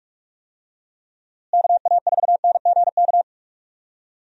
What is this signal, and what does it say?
Morse code, heard as a keyed tone, about 47 wpm, 700 Hz. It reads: KA4NGK
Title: USA Call Signs - 6 Characters - Repeated 3x - 45wpm